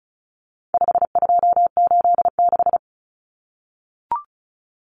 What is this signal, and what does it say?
Morse code, heard as a keyed tone, about 34 wpm, 700 Hz. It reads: 5286
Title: Numbers - 4-Digit 35wpm